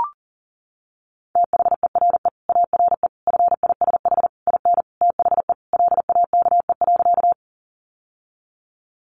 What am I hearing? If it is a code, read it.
Morse code, heard as a keyed tone, about 37 wpm, 700 Hz. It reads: THERE ARE FISH IN THE LAKE.
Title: Sentences from Top 1100 Words 40wpm